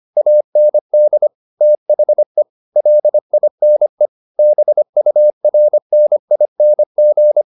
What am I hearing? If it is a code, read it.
Morse code, heard as a keyed tone, about 25 wpm, 600 Hz. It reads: AND THE LINE BURNING